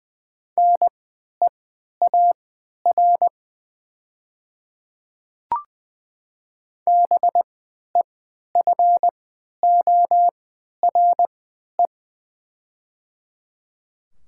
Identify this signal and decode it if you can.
Morse code, heard as a keyed tone, about 20 wpm, spaced as for 11 wpm, 700 Hz. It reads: NEAR BEFORE
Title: 3x Character Spacing - Top 300 Words - 20wpm